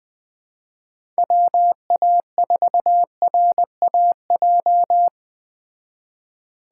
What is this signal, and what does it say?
Morse code, heard as a keyed tone, about 20 wpm, 700 Hz. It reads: WA4RAJ